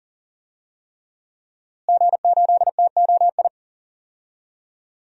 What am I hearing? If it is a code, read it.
Morse code, heard as a keyed tone, about 40 wpm, 700 Hz. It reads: G8TOI
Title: DXCC-100 Call Signs - 5 Characters - Repeated 3x - 40wpm